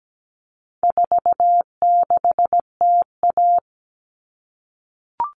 Morse code, heard as a keyed tone, about 17 wpm, 700 Hz. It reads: 46TA
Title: All Cut Numbers - 4-Digit 17wpm